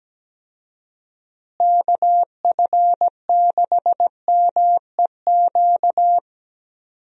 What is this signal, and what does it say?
Morse code, heard as a keyed tone, about 17 wpm, 700 Hz. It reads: KF6MEQ
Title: USA Call Signs - 6 Characters - Repeated 2x - 17wpm